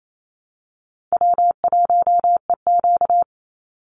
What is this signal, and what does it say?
Morse code, heard as a keyed tone, about 28 wpm, 700 Hz. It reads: W1EQ
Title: Call Signs Repeated 3x - 28wpm